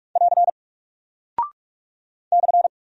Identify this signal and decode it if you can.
Morse code, heard as a keyed tone, about 44 wpm, 700 Hz. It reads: <AR> /
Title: Single Character (Letter:Number:Punctuation:Prosign) 45wpm